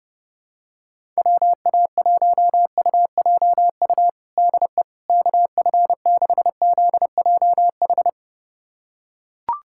Morse code, heard as a keyed tone, about 30 wpm, 700 Hz. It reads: WA1UJU DE KF6ZJH
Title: USA Callsign DE Callsign 30wpm